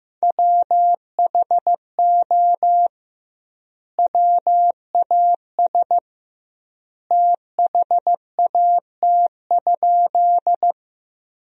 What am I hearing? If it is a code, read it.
Morse code, heard as a keyed tone, about 15 wpm, 700 Hz. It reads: WHO WAS THAT?
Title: 2x Word-Spacing - Sentences from Top 100 Words - 15wpm